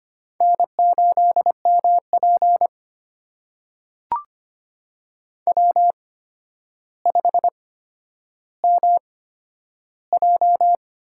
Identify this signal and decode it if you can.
Morse code, heard as a keyed tone, about 25 wpm, spaced as for 11 wpm, 700 Hz. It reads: N8MP W5MJ
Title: USA Call Signs - 4 Characters - 8x Character Spacing - 25wpm